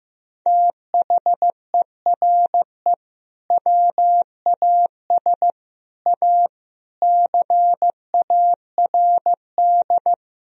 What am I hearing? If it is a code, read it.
Morse code, heard as a keyed tone, about 15 wpm, 700 Hz. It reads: THERE WAS A CARD
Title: Sentences from Top 1000 Words - Review Part 4 - 15wpm